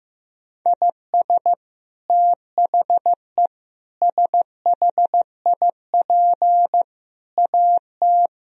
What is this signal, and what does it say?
Morse code, heard as a keyed tone, about 15 wpm, 700 Hz. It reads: IS THE SHIP AT
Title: Sentences from Top 1000 Words - Review Part 2 - 15wpm